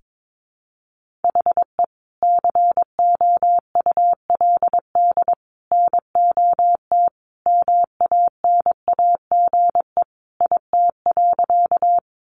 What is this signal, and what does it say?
Morse code, heard as a keyed tone, about 22 wpm, 700 Hz. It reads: HE COULD NOT MANAGE IT.
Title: Sentences from Top 1400 Words 22wpm